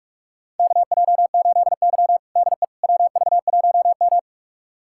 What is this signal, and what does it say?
Morse code, heard as a keyed tone, about 46 wpm, 700 Hz. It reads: KJ8Y DE WV1M
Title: USA Callsign DE Callsign - 4 Characters - 45wpm